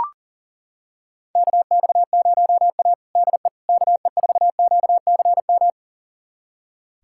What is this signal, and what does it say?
Morse code, heard as a keyed tone, about 40 wpm, 700 Hz. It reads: KX0A DE KE4QCM
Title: USA Callsign DE Callsign 40wpm